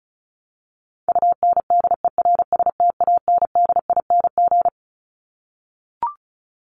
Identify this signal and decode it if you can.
Morse code, heard as a keyed tone, about 34 wpm, 700 Hz. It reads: UNDERSTANDING
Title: Top 501-750 Words 35wpm